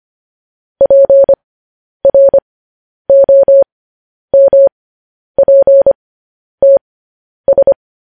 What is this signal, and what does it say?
Morse code, heard as a keyed tone, about 25 wpm, spaced as for 10 wpm, 550 Hz. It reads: PROMPTS
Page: news-headlines-in-morse-code-at-10-wpm